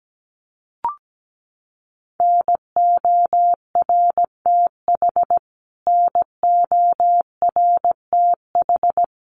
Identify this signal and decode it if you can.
Morse code, heard as a keyed tone, about 17 wpm, 700 Hz. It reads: NORTH NORTH
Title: Top 500 Words Repeated 17wpm